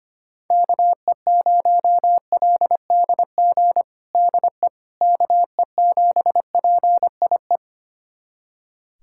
Morse code, heard as a keyed tone, about 25 wpm, 700 Hz. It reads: KE0LDG DE KE7PIE
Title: USA Callsign DE Callsign - 6 Characters - 25wpm